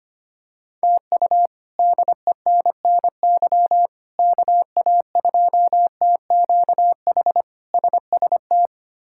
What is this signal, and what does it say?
Morse code, heard as a keyed tone, about 25 wpm, 700 Hz. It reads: TU DENNY KA2TQH SST
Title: K1USN SST QSO Practice - 25wpm